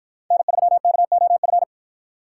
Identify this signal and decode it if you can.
Morse code, heard as a keyed tone, about 53 wpm, 700 Hz. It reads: N3KOF